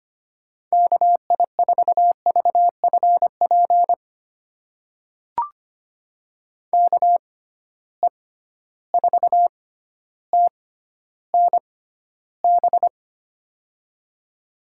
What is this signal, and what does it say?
Morse code, heard as a keyed tone, about 25 wpm, spaced as for 13 wpm, 700 Hz. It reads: KI4VFP KE4TNB
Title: USA Call Signs - 6 Characters - 6x Character Spacing - 25wpm